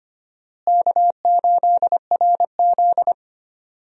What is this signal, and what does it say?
Morse code, heard as a keyed tone, about 25 wpm, 700 Hz. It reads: K8RZ
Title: USA Call Signs - 4 Characters - Repeated 3x - 25wpm